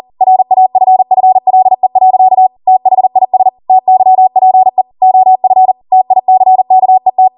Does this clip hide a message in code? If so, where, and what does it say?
Yes: Morse code from 0.2 to 7.3 s, heard as a keyed tone, about 40 wpm, 750 Hz. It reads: RAFFLE. THIS TYPE OF TICKET